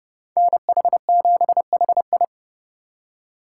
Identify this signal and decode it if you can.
Morse code, heard as a keyed tone, about 36 wpm, 700 Hz. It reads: NH7HI
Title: Speed Racing - USA Call Signs 30wpm